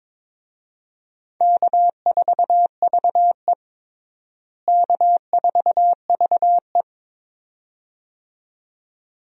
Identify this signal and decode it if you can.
Morse code, heard as a keyed tone, about 23 wpm, 700 Hz. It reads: K4VE K4VE
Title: USA Call Signs - 4 Characters - Repeated 2x - 22wpm